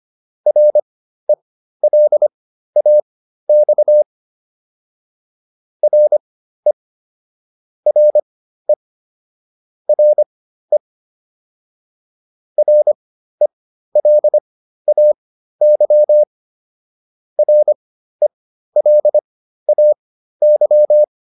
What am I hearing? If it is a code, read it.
Morse code, heard as a keyed tone, about 25 wpm, spaced as for 13 wpm, 600 Hz. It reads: RELAX RE RE RE RELAY RELAY